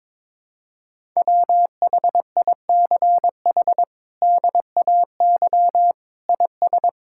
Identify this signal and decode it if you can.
Morse code, heard as a keyed tone, about 22 wpm, 700 Hz. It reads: WHICH DAY IS
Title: Sentences from Top 1000 Words - Review Part 5 - 22wpm